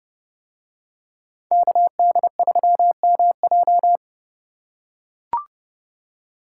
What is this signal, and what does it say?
Morse code, heard as a keyed tone, about 29 wpm, 700 Hz. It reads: KD3MJ